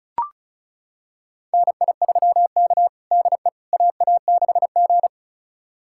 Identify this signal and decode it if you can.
Morse code, heard as a keyed tone, about 36 wpm, 700 Hz. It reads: NI3K DE AA6G
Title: USA Callsign DE Callsign - 4 Characters - 35wpm